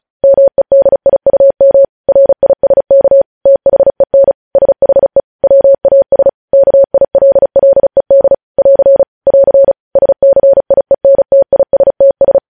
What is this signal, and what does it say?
Morse code, heard as a keyed tone, about 36 wpm, 550 Hz. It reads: MEDIUM RISK THEN SHE WAS KILLED <AR> <AR> SCIENTISTS
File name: podcast / News headlines-1x-35x35